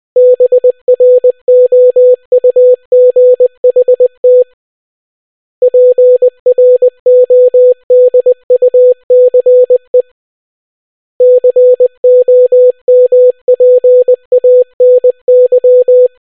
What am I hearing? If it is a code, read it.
Morse code, heard as a keyed tone, about 20 wpm, 500 Hz. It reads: BROUGHT PRODUCE COMPANY